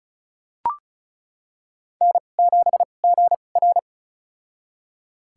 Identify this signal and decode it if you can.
Morse code, heard as a keyed tone, about 34 wpm, spaced as for 25 wpm, 700 Hz. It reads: N7GR
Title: USA Call Signs - 4 Characters - 2x Character Spacing - 35wpm